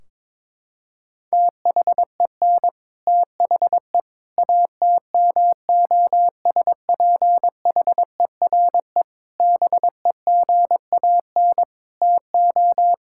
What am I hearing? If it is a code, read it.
Morse code, heard as a keyed tone, about 22 wpm, 700 Hz. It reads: THEN THE ATMOSPHERE BEGAN TO
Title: Sentences from Top 1600 Words 22wpm